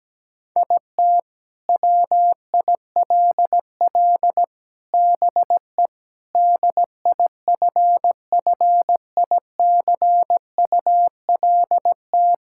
Morse code, heard as a keyed tone, about 17 wpm, 700 Hz. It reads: IT WILL BE DIFFICULT
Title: Sentences from Top 1000 Words - Review Part 6 - 17wpm